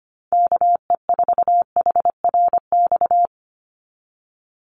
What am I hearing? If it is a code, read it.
Morse code, heard as a keyed tone, about 25 wpm, 700 Hz. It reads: KE4HRX
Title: USA Call Signs 25wpm